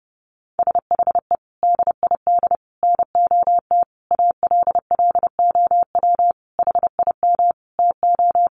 Morse code, heard as a keyed tone, about 28 wpm, 700 Hz. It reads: SHE DID NOT ALLOW HIM TO
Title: Sentences from Top 1200 Words 30wpm